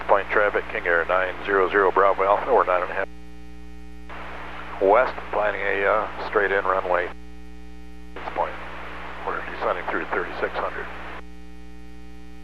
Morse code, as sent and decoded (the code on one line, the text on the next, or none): none